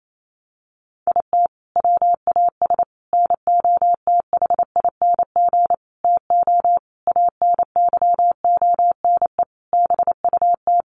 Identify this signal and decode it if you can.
Morse code, heard as a keyed tone, about 28 wpm, 700 Hz. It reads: IT WAS NOTHING TO ANYONE BUT
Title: Sentences from Top 1300 Words 28wpm